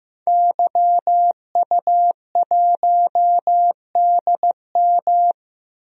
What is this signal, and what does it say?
Morse code, heard as a keyed tone, about 15 wpm, 700 Hz. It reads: YU1DM
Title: DXCC-100 Call Signs - 5 Characters - Repeated 3x - 15wpm